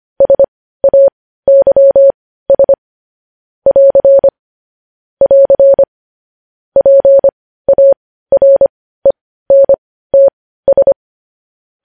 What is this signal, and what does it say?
Morse code, heard as a keyed tone, about 25 wpm, spaced as for 15 wpm, 550 Hz. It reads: SAYS <AR> <AR> PARENTS
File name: podcast / News headlines-1x-25x15